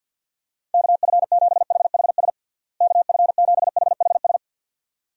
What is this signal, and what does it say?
Morse code, heard as a keyed tone, about 50 wpm, 700 Hz. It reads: KF7HHS KF7HHS
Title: USA Call Signs - 6 Characters - Repeated 2x - 50wpm